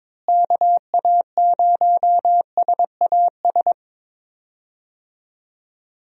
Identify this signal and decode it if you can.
Morse code, heard as a keyed tone, about 22 wpm, 700 Hz. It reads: KA0SAS
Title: USA Call Signs - 6 Characters - Repeated 3x - 22wpm